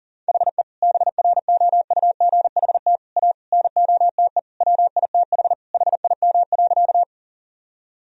Morse code, heard as a keyed tone, about 40 wpm, 700 Hz. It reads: HE BROUGHT A NOTE WITH HIM.
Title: Sentences from Top 1000 Words - Review Part 2 - 40wpm